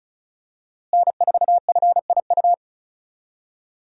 Morse code, heard as a keyed tone, about 36 wpm, 700 Hz. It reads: N4FIU
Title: USA Call Signs - 5 Characters - Repeated 2x - 35wpm